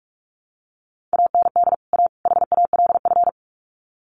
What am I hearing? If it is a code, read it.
Morse code, heard as a keyed tone, about 44 wpm, 700 Hz. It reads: AND A HALF
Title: Sets of 3 Words 45wpm